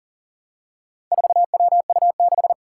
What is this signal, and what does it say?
Morse code, heard as a keyed tone, about 37 wpm, 700 Hz. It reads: 4WU6
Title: Four Challenging Letter-Number Combinations 40wpm